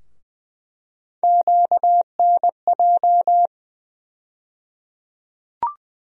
Morse code, heard as a keyed tone, about 21 wpm, 700 Hz. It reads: QNJ
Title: USA Call Sign Postfixes - 3 Letters 20wpm